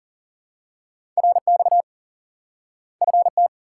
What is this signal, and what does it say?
Morse code, heard as a keyed tone, about 40 wpm, 700 Hz. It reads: RX FT
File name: IQE Course 3 - Lesson 89 - YAGI - Two Elements - 40wpm